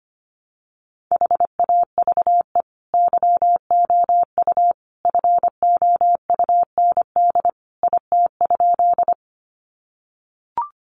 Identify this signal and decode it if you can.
Morse code, heard as a keyed tone, about 25 wpm, 700 Hz. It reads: HAVE YOU FOUND IT?
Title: Sentences from Top 300 Words 25wpm